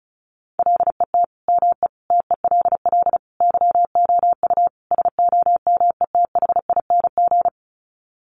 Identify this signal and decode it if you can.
Morse code, heard as a keyed tone, about 34 wpm, 700 Hz. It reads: LET ME TELL YOU SOMETHING
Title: Sets of 5 Words 35wpm